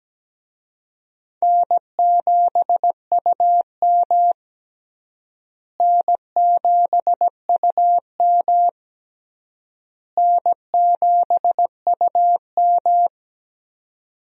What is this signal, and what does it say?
Morse code, heard as a keyed tone, about 17 wpm, 700 Hz. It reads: N7UM N7UM N7UM
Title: USA Call Signs - 4 Characters - Repeated 3x - 17wpm